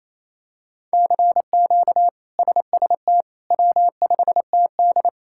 Morse code, heard as a keyed tone, about 28 wpm, 700 Hz. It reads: CQ SST W5TD
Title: K1USN SST QSO Practice - 28wpm